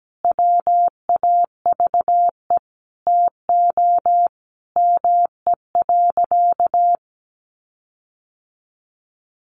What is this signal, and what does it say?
Morse code, heard as a keyed tone, about 17 wpm, 700 Hz. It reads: WAVE TO ME.
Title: Sentences from Top 1400 Words 17wpm